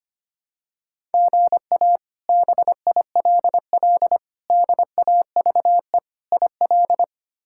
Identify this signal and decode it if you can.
Morse code, heard as a keyed tone, about 25 wpm, 700 Hz. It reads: GA BILL DAVE IL
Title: K1USN SST QSO Practice - 25wpm